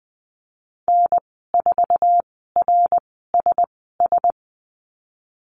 Morse code, heard as a keyed tone, about 20 wpm, spaced as for 15 wpm, 700 Hz. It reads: N4RSS
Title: 207 Lesson 33g - Letter j - Call Signs 20wpm